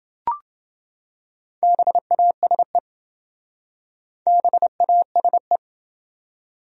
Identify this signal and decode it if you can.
Morse code, heard as a keyed tone, about 28 wpm, 700 Hz. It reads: BASE BASE